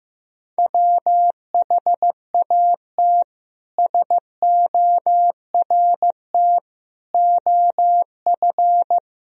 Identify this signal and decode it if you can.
Morse code, heard as a keyed tone, about 15 wpm, 700 Hz. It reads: WHAT SORT OF
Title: Sentences from Top 1000 Words - Review Part 3 - 15wpm